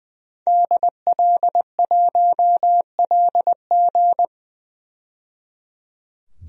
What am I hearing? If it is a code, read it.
Morse code, heard as a keyed tone, about 20 wpm, 700 Hz. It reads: DL1LG
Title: DXCC-100 Call Signs - 5 Characters - 20wpm